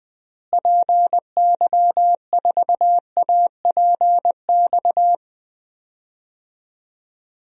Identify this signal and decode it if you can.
Morse code, heard as a keyed tone, about 20 wpm, 700 Hz. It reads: PY4APX
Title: DXCC-100 Call Signs - 6 Characters - Repeated 3x - 20wpm